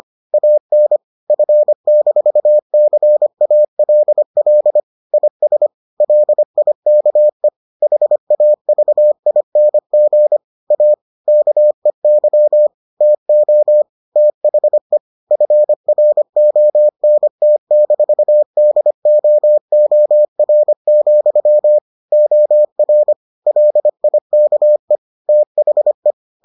Morse code, heard as a keyed tone, about 25 wpm, 600 Hz. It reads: AN F-CALL IS LIKE HAVING A KEY TO THE FRONT-DOOR, OR LIKE THE